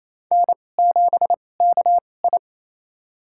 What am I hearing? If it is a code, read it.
Morse code, heard as a keyed tone, about 28 wpm, spaced as for 20 wpm, 700 Hz. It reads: N7KI
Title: USA Call Signs - 4 Characters - 2x Character Spacing - 28wpm